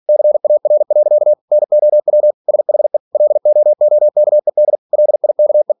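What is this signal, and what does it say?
Morse code, heard as a keyed tone, about 46 wpm, 600 Hz. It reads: CAR. NOW SHE LOOKED LIKE
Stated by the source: Practice text from an English novel sent at 47 WPM.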